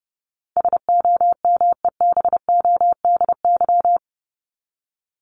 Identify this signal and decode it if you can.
Morse code, heard as a keyed tone, about 28 wpm, 700 Hz. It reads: SOMEBODY
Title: Top 1751-2000 Words 30wpm